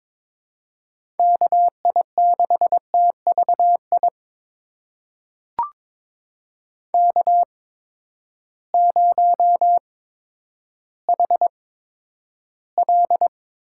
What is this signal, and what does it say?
Morse code, heard as a keyed tone, about 22 wpm, spaced as for 11 wpm, 700 Hz. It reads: KI6TVI K0HL
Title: USA Call Signs - 8x Character Spacing - 22wpm